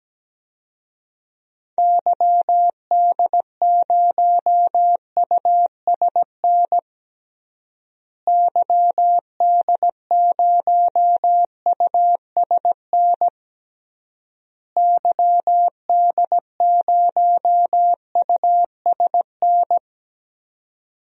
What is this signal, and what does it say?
Morse code, heard as a keyed tone, about 17 wpm, 700 Hz. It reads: YD0USN YD0USN YD0USN
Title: DXCC-100 Call Signs - 6 Characters - Repeated 3x - 17wpm